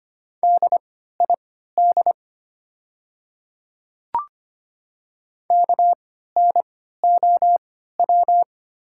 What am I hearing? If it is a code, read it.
Morse code, heard as a keyed tone, about 25 wpm, spaced as for 14 wpm, 700 Hz. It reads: DID KNOW
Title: 3x Character Spacing - Top 100 Words - 25wpm